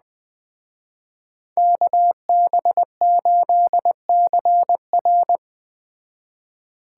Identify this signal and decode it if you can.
Morse code, heard as a keyed tone, about 20 wpm, 700 Hz. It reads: KB8CR
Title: Speed Racing - USA 2x2 Call Signs 17wpm